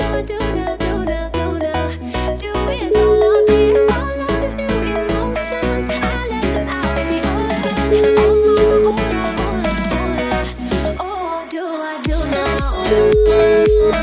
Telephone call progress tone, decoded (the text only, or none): ringback tone